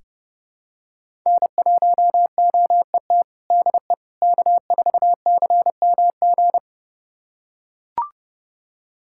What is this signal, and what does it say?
Morse code, heard as a keyed tone, about 30 wpm, 700 Hz. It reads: N1OET DE K4CMG
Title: USA Callsign DE Callsign - 5 Characters - 30wpm